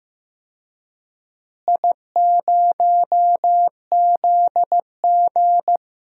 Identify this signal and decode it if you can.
Morse code, heard as a keyed tone, about 15 wpm, 700 Hz. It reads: I0ZG
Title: DXCC-100 Call Signs - 4 Characters - Repeated 3x - 15wpm